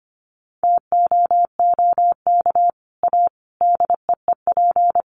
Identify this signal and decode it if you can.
Morse code, heard as a keyed tone, about 25 wpm, 700 Hz. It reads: TOOK A DEEP
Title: Sets of 5 Words Encore 25wpm